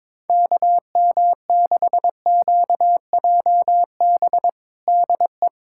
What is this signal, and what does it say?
Morse code, heard as a keyed tone, about 22 wpm, 700 Hz. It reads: KM6QJB DE
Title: USA Callsign DE Callsign - 6 Characters - 22wpm